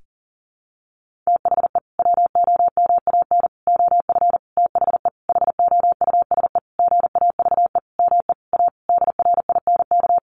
Morse code, heard as a keyed tone, about 40 wpm, 700 Hz. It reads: THE WOMAN OF THE HOUSE GAVE ME A DRINK
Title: Sentences from Top 1500 Words 40wpm